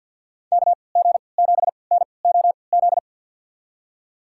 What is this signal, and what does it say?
Morse code, heard as a keyed tone, about 47 wpm, spaced as for 28 wpm, 700 Hz. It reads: KG7NOZ
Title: USA Call Signs - 6 Characters - 3x Character Spacing - 50wpm